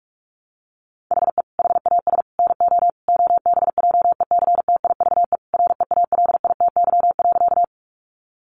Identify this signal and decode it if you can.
Morse code, heard as a keyed tone, about 44 wpm, 700 Hz. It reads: HE HAS NO OBJECTIVE REALITY.
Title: Sentences from Top 1200 Words 45wpm